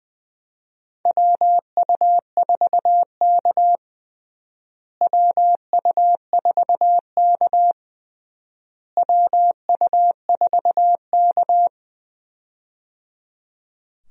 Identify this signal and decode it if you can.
Morse code, heard as a keyed tone, about 21 wpm, 700 Hz. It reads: WU4K WU4K WU4K
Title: USA Call Signs - 4 Characters - Repeated 3x - 20wpm